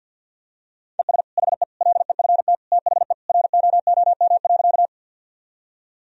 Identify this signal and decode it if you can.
Morse code, heard as a keyed tone, about 52 wpm, 700 Hz. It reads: ES HE LEFT THE ROOM.
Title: Sentences from Top 300 Words and Abbreviations 50wpm